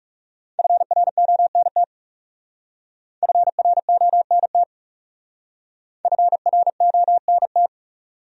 Morse code, heard as a keyed tone, about 38 wpm, 700 Hz. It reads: FRONT FRONT FRONT